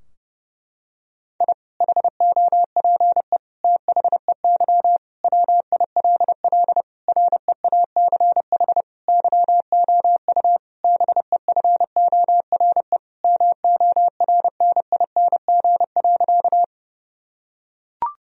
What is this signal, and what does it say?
Morse code, heard as a keyed tone, about 30 wpm, 700 Hz. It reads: I HOPE THEY WILL REACH YOU BEFORE MORNING.
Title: Sentences from Top 1000 Words - Review Part 2 - 30wpm